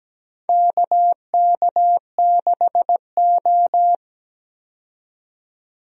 Morse code, heard as a keyed tone, about 17 wpm, 700 Hz. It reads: KK6O
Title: USA Call Signs - 4 Characters - 17wpm